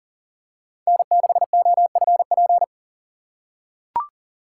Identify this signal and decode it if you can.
Morse code, heard as a keyed tone, about 37 wpm, 700 Hz. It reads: N6OFP
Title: USA Call Signs - 5 Characters - 4x Character Spacing - 40wpm